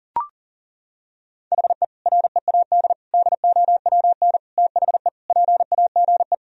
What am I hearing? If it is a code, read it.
Morse code, heard as a keyed tone, about 37 wpm, 700 Hz. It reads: HE READ DOWN THE PAGE
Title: Sentences from Top 1000 Words - Review Part 1 - 40wpm